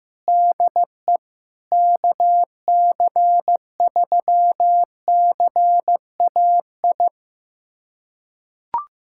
Morse code, heard as a keyed tone, about 15 wpm, 700 Hz. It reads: DE KC3CAI
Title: USA Callsign DE Callsign - 6 Characters - 15wpm